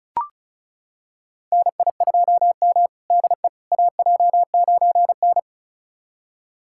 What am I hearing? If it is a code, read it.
Morse code, heard as a keyed tone, about 36 wpm, 700 Hz. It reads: NI2M DE AJ9N
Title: USA Callsign DE Callsign - 4 Characters - 35wpm